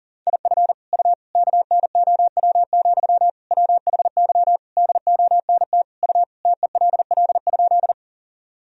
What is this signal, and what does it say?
Morse code, heard as a keyed tone, about 40 wpm, 700 Hz. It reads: IF U KNOW, WHY DONT U TELL?
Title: Sentences from Top 300 Words and Abbreviations 40wpm